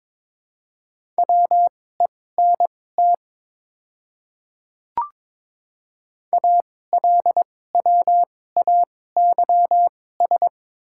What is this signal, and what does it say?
Morse code, heard as a keyed tone, about 22 wpm, spaced as for 16 wpm, 700 Hz. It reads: WENT ALWAYS
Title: 2x Character Spacing - Top 500 Words - 22wpm